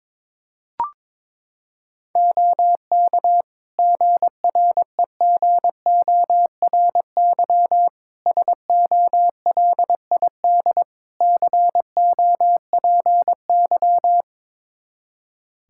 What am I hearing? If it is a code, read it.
Morse code, heard as a keyed tone, about 21 wpm, 700 Hz. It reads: OK GREGORY SOLID COPY
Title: Practice QSO - Exchange 3 - More Info - 22wpm